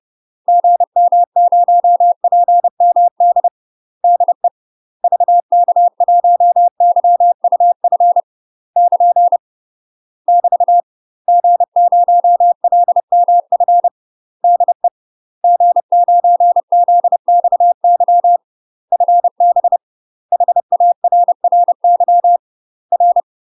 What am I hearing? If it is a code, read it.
Morse code, heard as a keyed tone, about 30 wpm, 700 Hz. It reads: GM0PMD DE VK1YUF <KN> <BT> G0LMF DE G9ZXY FB HARRY R
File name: UK MASTER QSOs TRAINING / 30WPM / UK MASTER QSOs 0041